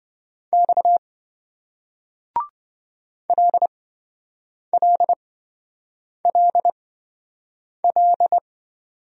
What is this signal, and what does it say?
Morse code, heard as a keyed tone, about 27 wpm, spaced as for 7 wpm, 700 Hz. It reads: X LLLL